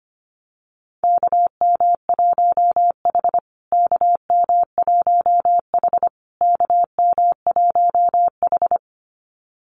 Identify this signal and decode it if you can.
Morse code, heard as a keyed tone, about 25 wpm, 700 Hz. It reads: KM1H KM1H KM1H
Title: USA Call Signs Repeated 3x 25wpm